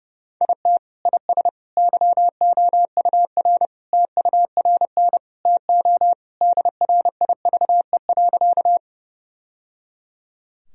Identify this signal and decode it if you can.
Morse code, heard as a keyed tone, about 32 wpm, 700 Hz. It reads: IT IS YOUR TURN TO DRIVE.
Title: Sentences from Top 1000 Words - Review Part 2 - 30wpm